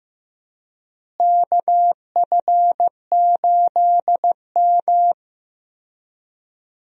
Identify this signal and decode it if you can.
Morse code, heard as a keyed tone, about 15 wpm, 700 Hz. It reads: KF8M
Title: USA Call Signs - 4 Characters - Repeated 2x - 15wpm